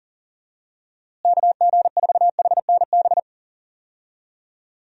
Morse code, heard as a keyed tone, about 40 wpm, 700 Hz. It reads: KG4HNB
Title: USA Call Signs - 6 Characters - Repeated 3x - 40wpm